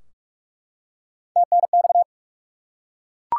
Morse code, heard as a keyed tone, about 46 wpm, 700 Hz. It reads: TNX
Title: Top 180 QSO Elements 45wpm